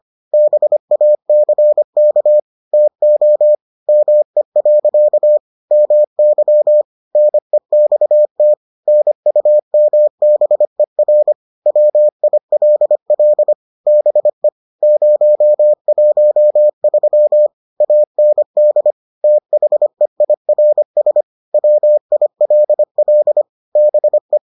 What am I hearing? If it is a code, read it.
Morse code, heard as a keyed tone, about 25 wpm, 600 Hz. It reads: BACK TO ME. MY NEXT NUMBER WILL BE 013 AND THEIRS WILL BE